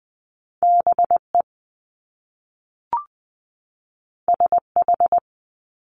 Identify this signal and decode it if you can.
Morse code, heard as a keyed tone, about 20 wpm, 700 Hz. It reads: BE SH
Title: Top Two Letter Combinations in Words 20wpm